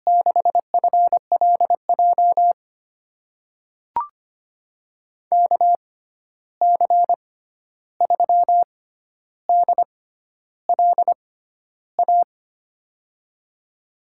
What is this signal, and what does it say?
Morse code, heard as a keyed tone, about 25 wpm, spaced as for 11 wpm, 700 Hz. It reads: 6FLJ KC3DLA
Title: USA Call Signs - 6 Characters - 6x Character Spacing - 25wpm